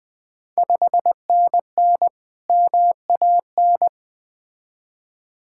Morse code, heard as a keyed tone, about 20 wpm, 700 Hz. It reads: 5NN MAN
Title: Ontario QSO Party Exchange 20wpm